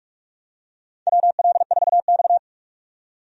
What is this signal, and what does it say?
Morse code, heard as a keyed tone, about 41 wpm, 700 Hz. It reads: WL4X
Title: USA Call Signs - 4 Characters - 45wpm